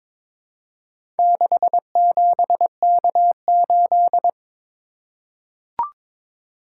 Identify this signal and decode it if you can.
Morse code, heard as a keyed tone, about 22 wpm, 700 Hz. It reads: 67K8
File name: practice / 22wpm / Four Letter-Number Combinations - 22wpm